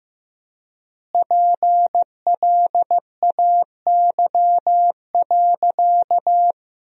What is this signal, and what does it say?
Morse code, heard as a keyed tone, about 15 wpm, 700 Hz. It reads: PLAY.
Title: 3x Word-Spacing - Sentences from Top 200 Words - 15wpm